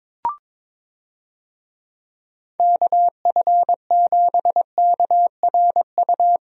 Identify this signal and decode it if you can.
Morse code, heard as a keyed tone, about 22 wpm, 700 Hz. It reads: KF7KRU
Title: USA Call Signs - 6 Characters - Repeated 3x - 22wpm